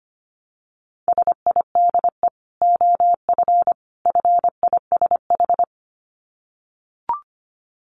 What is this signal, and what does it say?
Morse code, heard as a keyed tone, about 25 wpm, 700 Hz. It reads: SIDE OF FISH
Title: Dit Challenge 25wpm